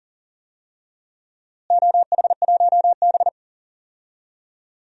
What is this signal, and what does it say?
Morse code, heard as a keyed tone, about 40 wpm, 700 Hz. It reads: OH1B
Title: DXCC-100 Call Signs - 4 Characters - Repeated 3x - 40wpm